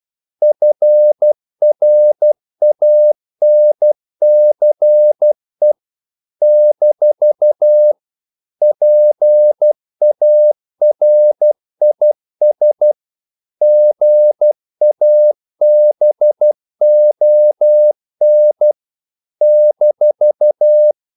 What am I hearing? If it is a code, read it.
Morse code, heard as a keyed tone, about 12 wpm, 600 Hz. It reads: FRANCE - PARIS GABON -
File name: CW / Practice / NationalCapitals12